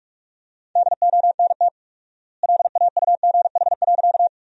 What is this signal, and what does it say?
Morse code, heard as a keyed tone, about 46 wpm, 690 Hz. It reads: DONT LAUGH.
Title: 4x Word-Spacing - Sentences from Top 500 Words - 45wpm